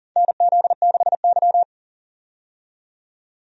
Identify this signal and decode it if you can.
Morse code, heard as a keyed tone, about 40 wpm, 700 Hz. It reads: NZ6Y
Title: USA Call Signs - 4 Characters - 40wpm